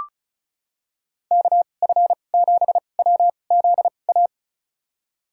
Morse code, heard as a keyed tone, about 34 wpm, spaced as for 25 wpm, 700 Hz. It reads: KF7WZA
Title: USA Call Signs - 6 Characters - 2x Character Spacing - 35wpm